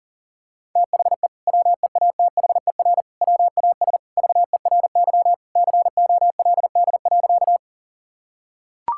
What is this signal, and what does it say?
Morse code, heard as a keyed tone, about 40 wpm, 700 Hz. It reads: THE WEATHER WAS VERY COLD.
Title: Sentences from Top 1000 Words - Review Part 5 - 40wpm